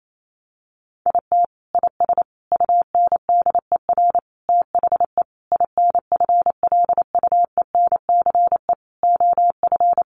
Morse code, heard as a keyed tone, about 28 wpm, 700 Hz. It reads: IT IS UNDER THE INFLUENCE OF
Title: Sentences from Top 1400 Words 28wpm